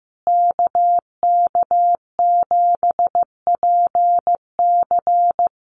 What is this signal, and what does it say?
Morse code, heard as a keyed tone, about 15 wpm, 700 Hz. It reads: KK7PC
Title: USA Call Signs Repeated 3x 15wpm